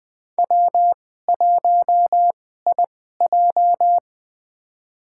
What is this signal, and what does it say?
Morse code, heard as a keyed tone, about 20 wpm, spaced as for 14 wpm, 700 Hz. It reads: W1IJ
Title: USA Call Signs - 4 Characters - 2x Character Spacing - 20wpm